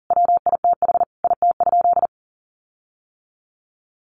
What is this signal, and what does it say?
Morse code, heard as a keyed tone, about 40 wpm, 700 Hz. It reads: WITH IT?
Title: Sentences from Top 1400 Words 40wpm